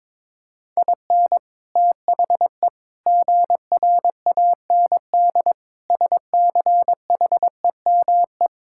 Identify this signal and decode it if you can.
Morse code, heard as a keyed tone, about 22 wpm, 700 Hz. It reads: IN THE GRAND SCHEME
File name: practice / 22wpm / Sets of 6 Words - 22wpm